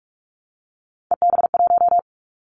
Morse code, heard as a keyed tone, about 44 wpm, 700 Hz. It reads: EB1
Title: Callsign Prefixes for Top 25 DXCC - 45wpm